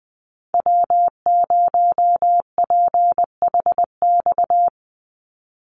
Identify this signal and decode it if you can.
Morse code, heard as a keyed tone, about 20 wpm, 700 Hz. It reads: W0PHX
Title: Call Signs Repeated 3x - 20wpm